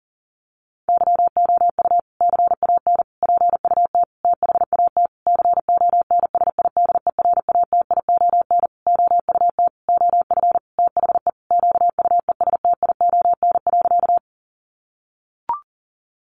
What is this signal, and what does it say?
Morse code, heard as a keyed tone, about 40 wpm, 700 Hz. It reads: YOU CAN PUT THAT CONSIDERATION OUT OF THE QUESTION.
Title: Sentences from Top 1500 Words 40wpm